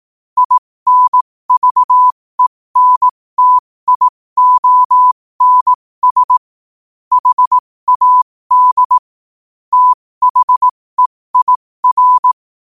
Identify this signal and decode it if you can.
Morse code, heard as a keyed tone, about 18 wpm, spaced as for 15 wpm, 1000 Hz. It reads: INVENTIONS HAD THEIR
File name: qotdfiles / 15wpm